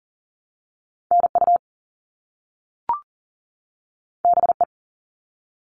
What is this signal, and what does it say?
Morse code, heard as a keyed tone, about 38 wpm, 700 Hz. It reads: NU BE
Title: All Cut Numbers - 2-Digit 40wpm